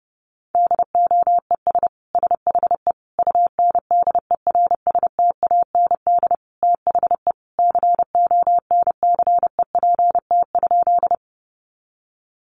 Morse code, heard as a keyed tone, about 30 wpm, 700 Hz. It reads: DOES SHE UNDERSTAND THE CONCEPT?
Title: Sentences from Top 1100 Words 30wpm